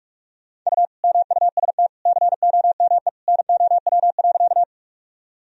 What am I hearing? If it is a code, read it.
Morse code, heard as a keyed tone, about 46 wpm, 700 Hz. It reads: U MUST COME NOW.
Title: Sentences from Top 200 Words and Abbreviations 45wpm